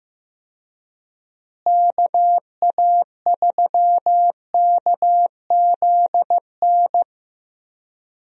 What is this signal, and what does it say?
Morse code, heard as a keyed tone, about 15 wpm, 700 Hz. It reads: KA3KZN
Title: USA Call Signs - 6 Characters - Repeated 3x - 15wpm